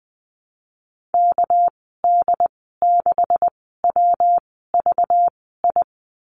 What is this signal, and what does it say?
Morse code, heard as a keyed tone, about 19 wpm, spaced as for 14 wpm, 700 Hz. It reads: KD6WVI
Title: 200 Lesson 32g - Letter k - Call Signs 20wpm